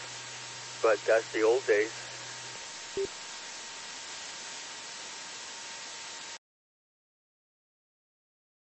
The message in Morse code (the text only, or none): none